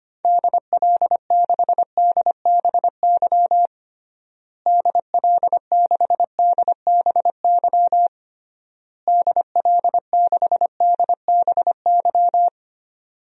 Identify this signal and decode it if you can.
Morse code, heard as a keyed tone, about 25 wpm, 700 Hz. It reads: DL6DBY DL6DBY DL6DBY
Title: DXCC-100 Call Signs - 6 Characters - Repeated 3x - 25wpm